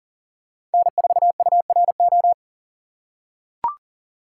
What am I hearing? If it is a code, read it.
Morse code, heard as a keyed tone, about 40 wpm, 700 Hz. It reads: N4URO
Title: USA Call Signs - 5 Characters - 2x Character Spacing - 40wpm